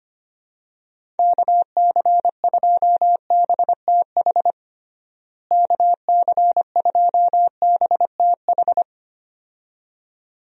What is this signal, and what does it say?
Morse code, heard as a keyed tone, about 25 wpm, 700 Hz. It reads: KC2BTH KC2BTH
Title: USA Call Signs - 6 Characters - Repeated 2x - 25wpm